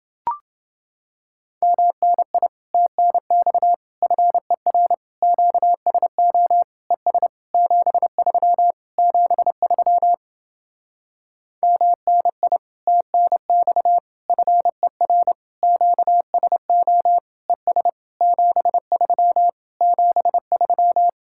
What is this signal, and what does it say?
Morse code, heard as a keyed tone, about 30 wpm, 700 Hz. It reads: MNI TNX FER QSO ES 73 73 MNI TNX FER QSO ES 73 73